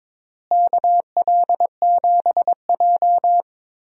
Morse code, heard as a keyed tone, about 22 wpm, 700 Hz. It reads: KL7J
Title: DXCC-100 Call Signs - 4 Characters - Repeated 3x - 22wpm